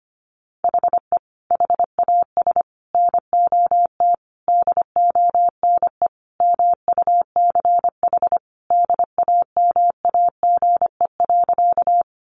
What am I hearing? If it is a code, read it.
Morse code, heard as a keyed tone, about 25 wpm, 700 Hz. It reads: HE HAS NOT DONE MUCH DAMAGE.
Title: Sentences from Top 1100 Words 25wpm